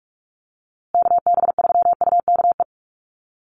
Morse code, heard as a keyed tone, about 44 wpm, 700 Hz. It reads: KB3UKE
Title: USA Call Signs Repeated 2x 45wpm